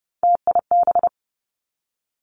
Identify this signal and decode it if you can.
Morse code, heard as a keyed tone, about 30 wpm, 700 Hz. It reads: TIB
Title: Three Challenging Letter-Number Combinations 30wpm